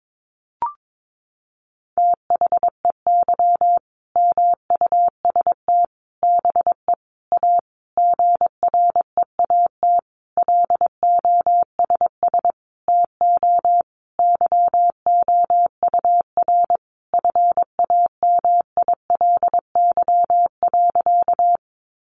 Morse code, heard as a keyed tone, about 22 wpm, 700 Hz. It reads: THEY MUST BE A GREAT LOSS TO YOUR FAMILY.
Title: Sentences from Top 1400 Words 22wpm